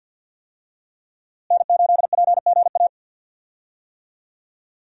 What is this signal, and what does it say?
Morse code, heard as a keyed tone, about 52 wpm, 700 Hz. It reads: N8PGA
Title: USA Call Signs - 5 Characters - Repeated 2x - 50wpm